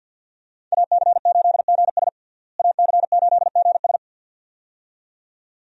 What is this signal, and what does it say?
Morse code, heard as a keyed tone, about 52 wpm, 700 Hz. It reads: AC8GS AC8GS
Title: USA Call Signs - 5 Characters - Repeated 2x - 50wpm